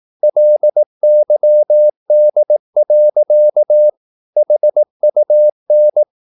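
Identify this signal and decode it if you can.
Morse code, heard as a keyed tone, about 18 wpm, 600 Hz. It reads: LYD. HUN
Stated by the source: Slow (QRS) Morse practice at 18 wpm, CW med Gnister.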